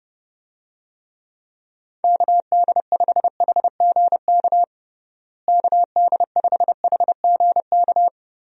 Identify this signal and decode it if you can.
Morse code, heard as a keyed tone, about 30 wpm, 700 Hz. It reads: KD5HGK KD5HGK
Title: USA Call Signs - 6 Characters - Repeated 3x - 30wpm